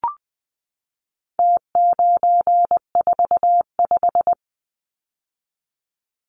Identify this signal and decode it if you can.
Morse code, heard as a keyed tone, about 20 wpm, 700 Hz. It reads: T945
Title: Numbers - Contest Style - 4-Digit 20wpm